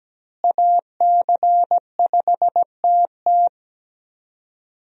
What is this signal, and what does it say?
Morse code, heard as a keyed tone, about 17 wpm, 700 Hz. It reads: AC5TT
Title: USA Call Signs - 5 Characters - Repeated 3x - 17wpm